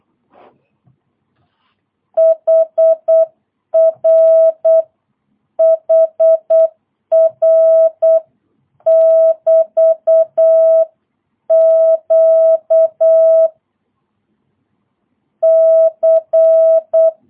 Morse code, heard as a keyed tone, about 8 wpm, 650 Hz. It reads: HRHR<BT>Q C
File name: jh8cbh / 2022cw